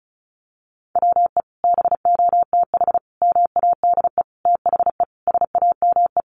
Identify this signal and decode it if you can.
Morse code, heard as a keyed tone, about 34 wpm, 700 Hz. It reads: WE BOTH MADE THE SAME
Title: Sentences from Top 1500 Words 35wpm